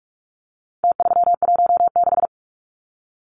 Morse code, heard as a keyed tone, about 44 wpm, 700 Hz. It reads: T316
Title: Numbers - Contest Style - 4-Digit 45wpm